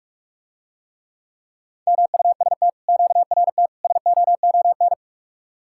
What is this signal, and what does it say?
Morse code, heard as a keyed tone, about 44 wpm, 700 Hz. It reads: MUST QRT SOON